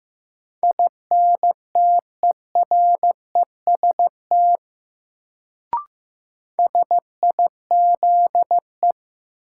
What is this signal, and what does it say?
Morse code, heard as a keyed tone, about 15 wpm, 700 Hz. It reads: INTEREST SIZE
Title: Sentences from Top 400 Words - New Word Drill - 15wpm